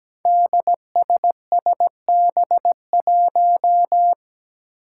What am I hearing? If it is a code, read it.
Morse code, heard as a keyed tone, about 17 wpm, 700 Hz. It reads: DSSB1
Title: Five Challenging Letter-Number Combinations 17wpm